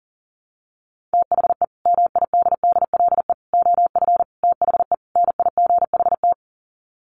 Morse code, heard as a keyed tone, about 40 wpm, 700 Hz. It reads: THE MIDDLE OF THE NIGHT
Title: Sets of 5 Words 40wpm